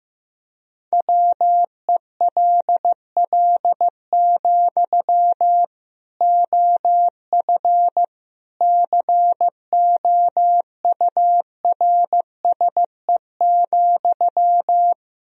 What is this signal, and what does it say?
Morse code, heard as a keyed tone, about 15 wpm, 700 Hz. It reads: WELL, OF COURSE,
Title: Sentences from Top 1000 Words - Review Part 1 - 15wpm